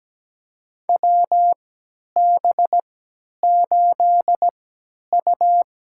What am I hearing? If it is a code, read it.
Morse code, heard as a keyed tone, about 17 wpm, spaced as for 10 wpm, 700 Hz. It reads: WB8U
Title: USA Call Signs - 3x Character Spacing - 17wpm